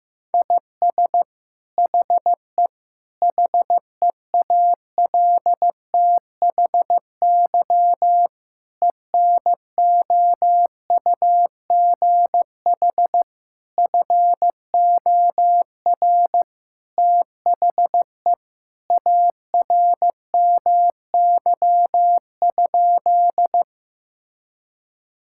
Morse code, heard as a keyed tone, about 15 wpm, 700 Hz. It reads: IS HE HEALTHY ENOUGH FOR THE ARMY?
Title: Sentences from Top 1000 Words - Review Part 5 - 15wpm